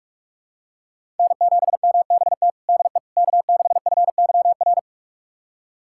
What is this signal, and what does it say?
Morse code, heard as a keyed tone, about 46 wpm, 700 Hz. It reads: N7MBT DE K6FYR
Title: USA Callsign DE Callsign - 5 Characters - 45wpm